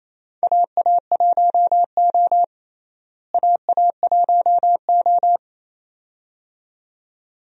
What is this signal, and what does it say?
Morse code, heard as a keyed tone, about 28 wpm, 700 Hz. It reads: AA1O AA1O
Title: USA Call Signs - 4 Characters - Repeated 2x - 28wpm